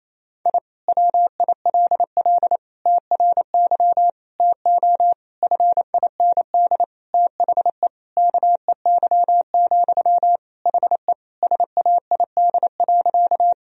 Morse code, heard as a keyed tone, about 27 wpm, 700 Hz. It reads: I WILL TRY TO FIND THE KEY, HE SAID.
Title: Sentences from Top 1000 Words - Review Part 4 - 28wpm